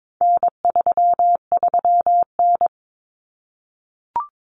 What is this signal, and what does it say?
Morse code, heard as a keyed tone, about 22 wpm, 700 Hz. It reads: N33N
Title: Common Cut Numbers - 4-Digit 22wpm